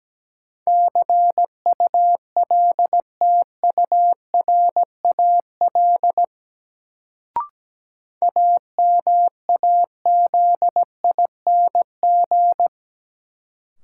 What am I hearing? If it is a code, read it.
Morse code, heard as a keyed tone, about 17 wpm, 700 Hz. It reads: CULTURAL AMAZING
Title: Sentences from Top 1500 Words - New Word Drill - 17wpm